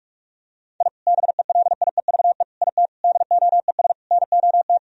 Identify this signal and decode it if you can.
Morse code, heard as a keyed tone, about 41 wpm, 700 Hz. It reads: I BELIEVE IT DOES NOT
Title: Sentences from Top 1000 Words - Review Part 2 - 45wpm